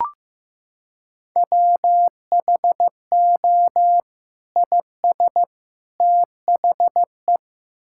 Morse code, heard as a keyed tone, about 15 wpm, 700 Hz. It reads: WHO IS THE
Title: Sentences from Top 1600 Words 15wpm